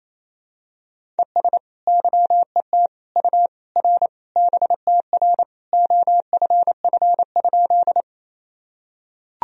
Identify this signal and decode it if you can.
Morse code, heard as a keyed tone, about 28 wpm, 700 Hz. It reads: ES YET U R BTR OFF?
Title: Sentences from Top 500 Words and Abbreviations 28wpm